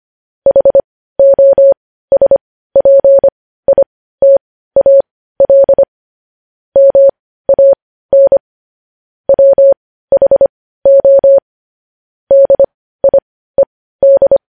Morse code, heard as a keyed tone, about 25 wpm, spaced as for 15 wpm, 550 Hz. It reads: HOSPITAL MAN WHO DIED
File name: podcast / News headlines-1x-25x15